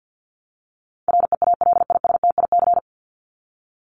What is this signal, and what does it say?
Morse code, heard as a keyed tone, about 50 wpm, 700 Hz. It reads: REALISTIC
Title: Top 1751-2000 Words 50wpm